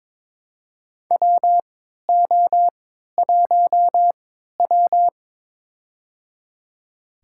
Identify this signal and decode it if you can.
Morse code, heard as a keyed tone, about 21 wpm, spaced as for 12 wpm, 700 Hz. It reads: WO1W
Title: USA Call Signs - 4 Characters - 3x Character Spacing - 22wpm